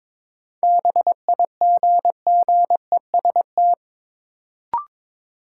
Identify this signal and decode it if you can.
Morse code, heard as a keyed tone, about 22 wpm, 700 Hz. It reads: BIGGEST
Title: Sets of 4 Words - New Word Drill - 22wpm